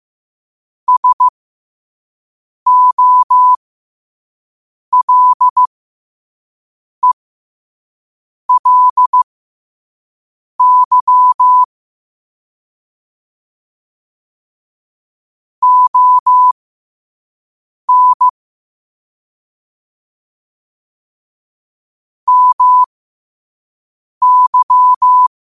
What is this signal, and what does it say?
Morse code, heard as a keyed tone, about 15 wpm, spaced as for 5 wpm, 1000 Hz. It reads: SOLELY ON MY
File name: qotdfiles / 5wpm